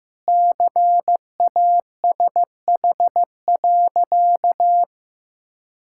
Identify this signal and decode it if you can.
Morse code, heard as a keyed tone, about 15 wpm, 700 Hz. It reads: CASH.
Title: Sentences from Top 1000 Words - Review Part 5 - 15wpm